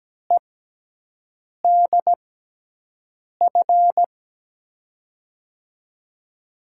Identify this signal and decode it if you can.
Morse code, heard as a keyed tone, about 17 wpm, spaced as for 6 wpm, 700 Hz. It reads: EDF
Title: USA Call Signs - 6 Characters - 6x Character Spacing - 17wpm